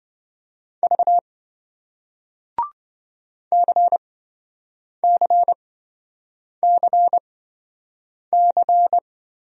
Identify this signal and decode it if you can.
Morse code, heard as a keyed tone, about 27 wpm, spaced as for 7 wpm, 700 Hz. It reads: V CCCC